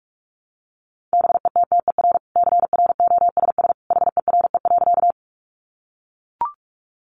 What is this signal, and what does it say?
Morse code, heard as a keyed tone, about 44 wpm, 700 Hz. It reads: BETTER CROSS HERE.
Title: Sentences from Top 400 Words 45wpm